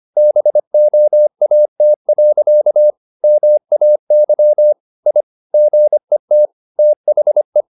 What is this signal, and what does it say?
Morse code, heard as a keyed tone, about 25 wpm, 600 Hz. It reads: BOAT. MAY I GET THE
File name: the old man and the sea part 1 25wpm